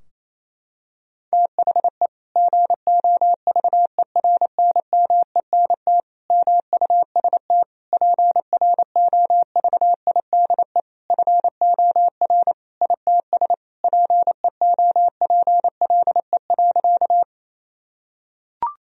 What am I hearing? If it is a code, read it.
Morse code, heard as a keyed tone, about 28 wpm, 700 Hz. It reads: THE GOVERNMENT MUST PROVIDE FOR ITS PEOPLE.
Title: Sentences from Top 1000 Words - Review Part 3 - 28wpm